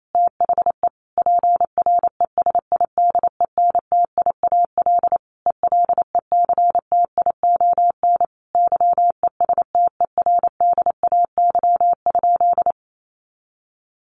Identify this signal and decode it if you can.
Morse code, heard as a keyed tone, about 28 wpm, 700 Hz. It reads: THE PRESIDENTIAL ELECTION YESTERDAY?
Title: A Sound Of Thunder 28wpm